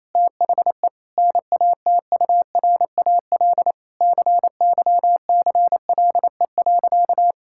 Morse code, heard as a keyed tone, about 27 wpm, 700 Hz. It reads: THE NATURAL CYCLE.
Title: Sentences from Top 1000 Words - Review Part 6 - 28wpm